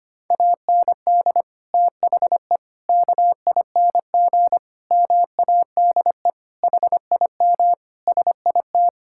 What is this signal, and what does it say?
Morse code, heard as a keyed tone, about 25 wpm, 700 Hz. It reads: AND THE KING MADE HIM SIT
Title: Sentences from Top 1600 Words 25wpm